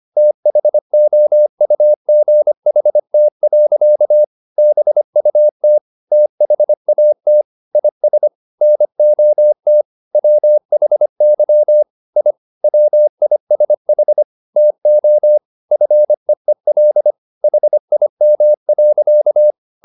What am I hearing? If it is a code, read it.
Morse code, heard as a keyed tone, about 25 wpm, 600 Hz. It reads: THOUGHT. BUT THAT IS NOT WHY I WISH TO FEEL HIM.